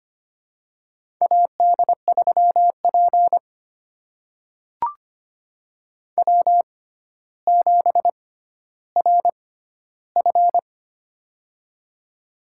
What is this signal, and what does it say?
Morse code, heard as a keyed tone, about 25 wpm, spaced as for 13 wpm, 700 Hz. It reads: AD3P W7RF
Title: USA Call Signs - 4 Characters - 6x Character Spacing - 25wpm